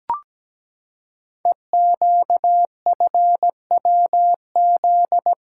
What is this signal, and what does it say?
Morse code, heard as a keyed tone, about 17 wpm, 700 Hz. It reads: EQFWZ
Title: Code Groups - 5 Letter:Numbers - 17wpm